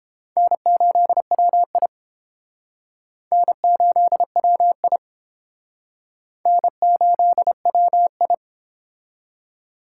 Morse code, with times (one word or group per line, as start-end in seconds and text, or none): 0.4-1.9 N8WI
3.3-5.0 N8WI
6.5-8.3 N8WI